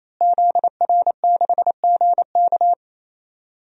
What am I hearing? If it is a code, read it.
Morse code, heard as a keyed tone, about 28 wpm, 700 Hz. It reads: ZR6GK
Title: DXCC-100 Call Signs - 5 Characters - Repeated 3x - 28wpm